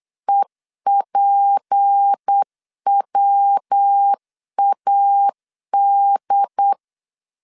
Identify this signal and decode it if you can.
Morse code, heard as a keyed tone, about 8 wpm, 800 Hz. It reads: EPWAD
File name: LEZIONI / 6 PKX0 / 6 P 8wpm 3025